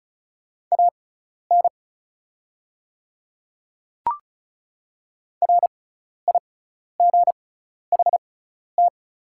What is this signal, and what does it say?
Morse code, heard as a keyed tone, about 34 wpm, spaced as for 12 wpm, 700 Hz. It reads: AN RIGHT
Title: 6x Character Spacing - Top 300 Words - 35wpm